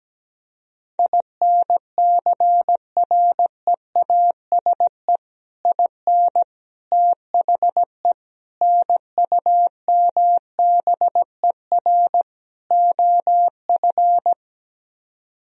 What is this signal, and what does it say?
Morse code, heard as a keyed tone, about 17 wpm, 700 Hz. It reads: INCREASE IN THE NUMBER OF
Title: Speed-Racing - Sets of 5 Words 17wpm